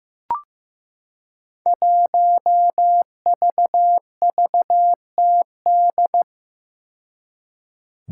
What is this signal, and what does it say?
Morse code, heard as a keyed tone, about 15 wpm, 700 Hz. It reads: 1VVTD
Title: Five Challenging Letter-Number Combinations 15wpm